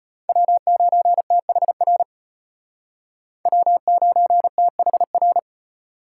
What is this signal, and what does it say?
Morse code, heard as a keyed tone, about 32 wpm, 700 Hz. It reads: W9THR W9THR